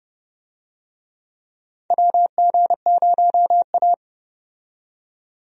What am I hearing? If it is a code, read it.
Morse code, heard as a keyed tone, about 31 wpm, 700 Hz. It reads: WG0A
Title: USA Call Signs - 4 Characters - 30wpm